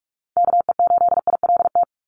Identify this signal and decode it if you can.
Morse code, heard as a keyed tone, about 46 wpm, 700 Hz. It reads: KE8ILT
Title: USA Call Signs 45wpm